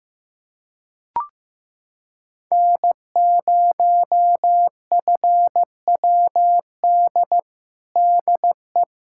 Morse code, heard as a keyed tone, about 15 wpm, 700 Hz. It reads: N0FWD DE
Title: USA Callsign DE Callsign - 5 Characters - 15wpm